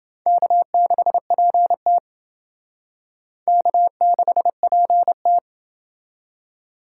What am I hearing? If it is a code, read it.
Morse code, heard as a keyed tone, about 28 wpm, 700 Hz. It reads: K6PT K6PT